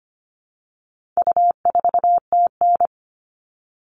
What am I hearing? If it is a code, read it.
Morse code, heard as a keyed tone, about 25 wpm, 700 Hz. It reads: U4TN
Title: All Cut Numbers - 4-Digit 25wpm